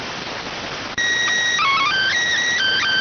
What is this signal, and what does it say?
SSTV (slow-scan television), PD 290.